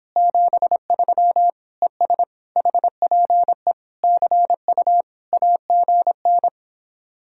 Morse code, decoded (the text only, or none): 73 ES HPE CU AGN